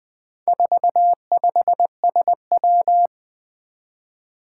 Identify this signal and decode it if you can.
Morse code, heard as a keyed tone, about 21 wpm, 700 Hz. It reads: 45SW
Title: Four Challenging Letter-Number Combinations 20wpm